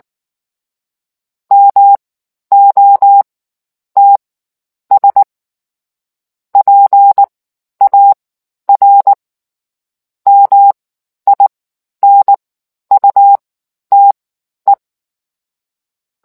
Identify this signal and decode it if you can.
Morse code, heard as a keyed tone, about 19 wpm, spaced as for 10 wpm, 800 Hz. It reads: MOTS PAR MINUTE